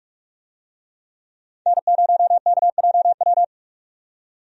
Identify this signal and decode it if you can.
Morse code, heard as a keyed tone, about 43 wpm, 700 Hz. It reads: N0KJW
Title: USA Call Signs - 5 Characters - 45wpm